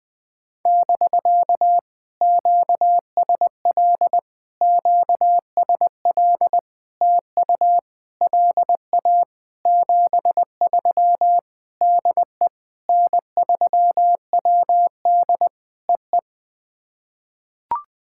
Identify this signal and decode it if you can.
Morse code, heard as a keyed tone, about 20 wpm, 700 Hz. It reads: <BK> QSL QSL TU LA 73 DE N3WD EE
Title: POTA QSO Practice - 20wpm